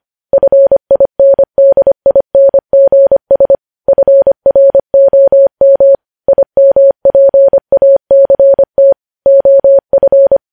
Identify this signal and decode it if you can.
Morse code, heard as a keyed tone, about 25 wpm, 550 Hz. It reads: FINDINGS FROM IMPACT OF
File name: podcast / News headlines-1x-25x25